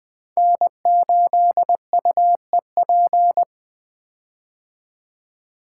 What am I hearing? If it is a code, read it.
Morse code, heard as a keyed tone, about 20 wpm, 700 Hz. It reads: N8UEP
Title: Code Groups - 5 Letter:Numbers - 20wpm